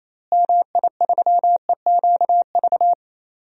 Morse code, heard as a keyed tone, about 28 wpm, 700 Hz. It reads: MI3EQV
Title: DXCC-100 Call Signs - 6 Characters - Repeated 3x - 28wpm